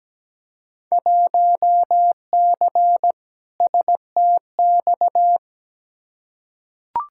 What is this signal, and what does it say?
Morse code, heard as a keyed tone, about 17 wpm, 700 Hz. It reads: 1C STX
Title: Field Day Exchanges Encore - 17wpm